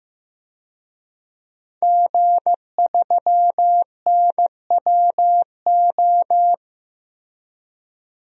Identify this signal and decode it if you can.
Morse code, heard as a keyed tone, about 16 wpm, 700 Hz. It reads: G3NWO
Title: DXCC-100 Call Signs - 5 Characters - 15wpm